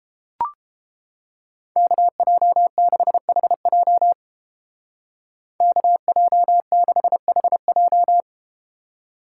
Morse code, heard as a keyed tone, about 33 wpm, 700 Hz. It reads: KJ6HJ KJ6HJ